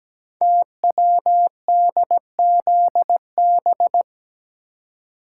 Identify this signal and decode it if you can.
Morse code, heard as a keyed tone, about 17 wpm, 700 Hz. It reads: TWDZB
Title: Code Groups - 5 Letters - 17wpm